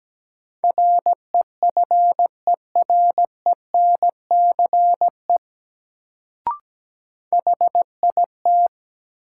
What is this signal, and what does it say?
Morse code, heard as a keyed tone, about 17 wpm, 700 Hz. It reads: REFERENCE HIT
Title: Sentences from Top 1200 Words - New Word Drill - 17wpm